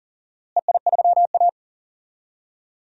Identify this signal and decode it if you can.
Morse code, heard as a keyed tone, about 40 wpm, 700 Hz. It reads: EI3A
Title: DXCC-100 Call Signs - 4 Characters - Repeated 2x - 40wpm